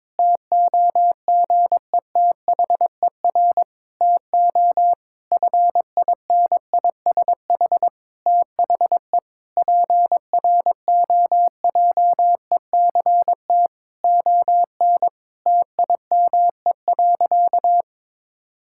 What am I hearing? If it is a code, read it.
Morse code, heard as a keyed tone, about 22 wpm, 700 Hz. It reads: TOGETHER TO FINISH THE PROJECT ON TIME.
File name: Difficult Sentences with Hints - 22wpm